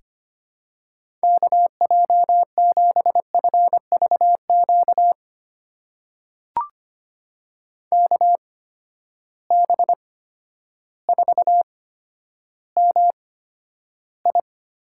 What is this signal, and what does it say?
Morse code, heard as a keyed tone, about 25 wpm, spaced as for 11 wpm, 700 Hz. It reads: KJ7FVQ KB4MI
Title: USA Call Signs - 6 Characters - 8x Character Spacing - 25wpm